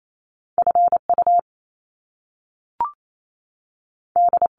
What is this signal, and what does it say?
Morse code, heard as a keyed tone, about 28 wpm, spaced as for 3 wpm, 700 Hz. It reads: F UD
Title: Top Two Letter Combinations in Words 28wpm